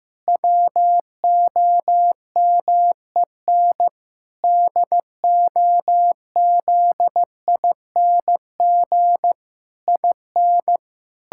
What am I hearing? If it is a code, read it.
Morse code, heard as a keyed tone, about 16 wpm, 700 Hz. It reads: WOMEN DOZING IN
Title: Pangram Challenge v2 - 15wpm